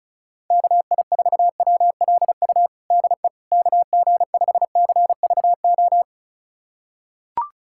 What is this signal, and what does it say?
Morse code, heard as a keyed tone, about 36 wpm, 690 Hz. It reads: KI4WLU DE KG5CVO
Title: USA Callsign DE Callsign - 6 Characters - 35wpm